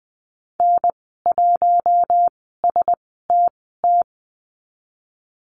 Morse code, heard as a keyed tone, about 20 wpm, spaced as for 15 wpm, 700 Hz. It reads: N1STT
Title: 076 Lesson 13g - Letter L - Call Signs 20wpm